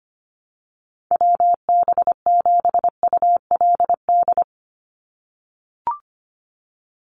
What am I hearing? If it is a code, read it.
Morse code, heard as a keyed tone, about 25 wpm, 700 Hz. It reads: WB7ULD
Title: NA SOTA Chaser Callsigns 25wpm